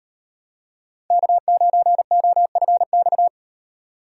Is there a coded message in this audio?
Yes: Morse code from 1.1 to 3.3 s, heard as a keyed tone, about 38 wpm, 700 Hz. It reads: K9OFX